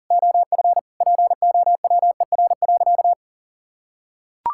Morse code, heard as a keyed tone, about 40 wpm, 700 Hz. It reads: OF POWER.
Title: Sentences from Top 1000 Words - Review Part 6 - 40wpm